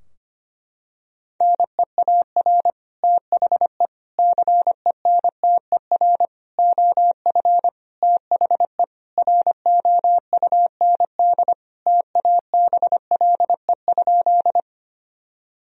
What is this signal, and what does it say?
Morse code, heard as a keyed tone, about 25 wpm, 700 Hz. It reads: NEAR THE CENTER OF THE ROUND TABLE?
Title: Sentences from Top 1000 Words - Review Part 2 - 25wpm